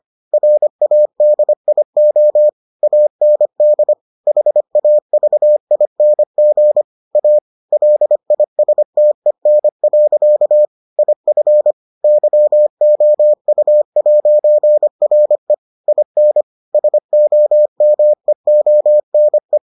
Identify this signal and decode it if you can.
Morse code, heard as a keyed tone, about 25 wpm, 600 Hz. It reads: RADIO AND HAVING A LISTEN. IF YOU'RE IN SOMEONE